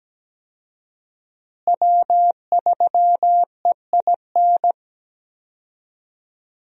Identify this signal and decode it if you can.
Morse code, heard as a keyed tone, about 17 wpm, 700 Hz. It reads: W3EIN
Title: USA Call Signs - 5 Characters - Repeated 3x - 17wpm